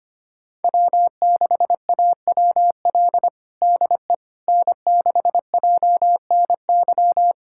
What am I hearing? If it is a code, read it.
Morse code, heard as a keyed tone, about 25 wpm, 700 Hz. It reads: W6AWL DE N6JNY
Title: USA Callsign DE Callsign - 5 Characters - 25wpm